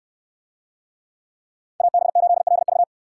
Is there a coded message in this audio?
Yes: Morse code from 1.8 to 2.8 s, heard as a keyed tone, about 68 wpm, 700 Hz. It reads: AB8LV